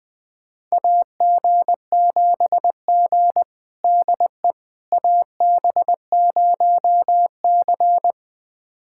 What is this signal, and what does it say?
Morse code, heard as a keyed tone, about 20 wpm, 700 Hz. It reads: AG7G DE AB0C
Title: USA Callsign DE Callsign - 4 Characters - 20wpm